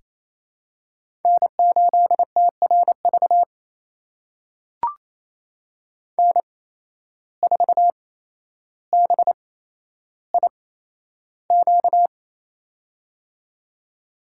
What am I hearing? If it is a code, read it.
Morse code, heard as a keyed tone, about 28 wpm, spaced as for 12 wpm, 700 Hz. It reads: N8TRV N4BIQ
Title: USA Call Signs - 5 Characters - 8x Character Spacing - 28wpm